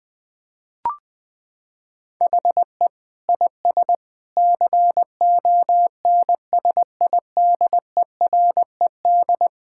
Morse code, heard as a keyed tone, about 21 wpm, 700 Hz. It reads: HE IS CONSIDERED
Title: Sentences with 7-Word Phrases - 20wpm